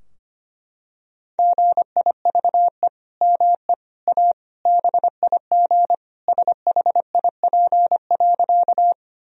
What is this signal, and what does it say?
Morse code, heard as a keyed tone, about 25 wpm, 700 Hz. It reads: GIVE ME A BIG SHIP.
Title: Sentences from Top 1000 Words - Review Part 2 - 25wpm